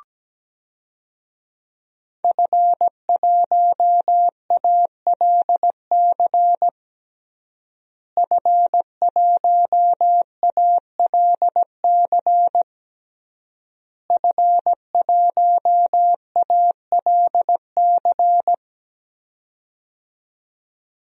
Morse code, heard as a keyed tone, about 17 wpm, 700 Hz. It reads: F1ALC F1ALC F1ALC
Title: DXCC-100 Call Signs - 5 Characters - Repeated 3x - 17wpm